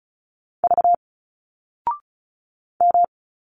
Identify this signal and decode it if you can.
Morse code, heard as a keyed tone, about 34 wpm, 700 Hz. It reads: V M
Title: Single Letters 35wpm